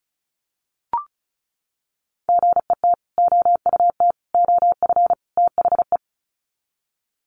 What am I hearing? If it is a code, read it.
Morse code, heard as a keyed tone, about 34 wpm, 700 Hz. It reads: GET OUT OF THE
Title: Sets of 4 Words 35wpm